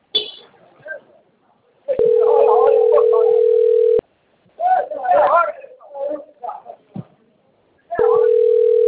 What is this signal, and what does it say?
Ringback tone (North America), 440 + 480 Hz, with speech.
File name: rg-1600-unknown- - - .59512